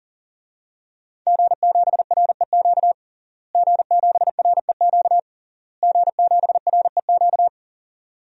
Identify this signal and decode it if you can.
Morse code, heard as a keyed tone, about 40 wpm, 700 Hz. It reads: G7REQ G7REQ G7REQ
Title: DXCC-100 Call Signs - 5 Characters - Repeated 3x - 40wpm